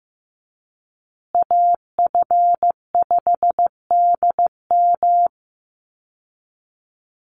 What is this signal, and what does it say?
Morse code, heard as a keyed tone, about 15 wpm, 700 Hz. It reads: AF5DM
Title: Call Signs Repeated 2x - 15wpm